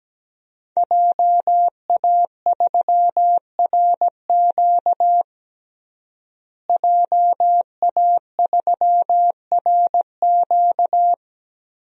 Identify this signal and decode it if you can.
Morse code, heard as a keyed tone, about 17 wpm, 700 Hz. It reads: JA3RQ JA3RQ
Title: DXCC-100 Call Signs - 5 Characters - Repeated 3x - 17wpm